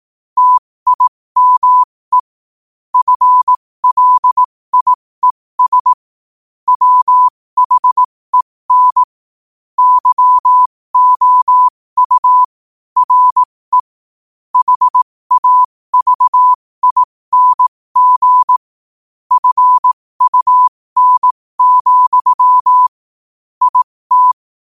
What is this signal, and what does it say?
Morse code, heard as a keyed tone, about 18 wpm, spaced as for 15 wpm, 1000 Hz. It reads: TIME FLIES WHEN YOU RE HAVING FUN, IT